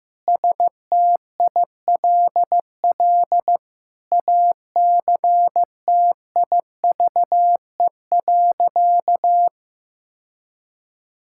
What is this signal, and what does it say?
Morse code, heard as a keyed tone, about 16 wpm, 700 Hz. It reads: STILL ACTIVE.
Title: Sentences from Top 1000 Words - Review Part 6 - 15wpm